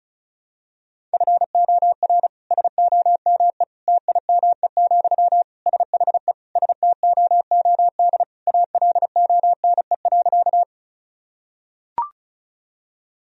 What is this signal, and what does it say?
Morse code, heard as a keyed tone, about 36 wpm, 700 Hz. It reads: FOR SOME TIME, SHE STOOD ALONE.
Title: Sentences from Top 1000 Words - Review Part 2 - 35wpm